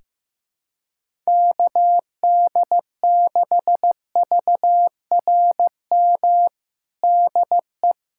Morse code, heard as a keyed tone, about 16 wpm, 700 Hz. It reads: KD6VRM DE
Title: USA Callsign DE Callsign 15wpm